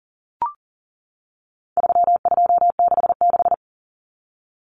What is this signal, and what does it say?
Morse code, heard as a keyed tone, about 40 wpm, 700 Hz. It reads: 3266
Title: Numbers - 4-Digit 40wpm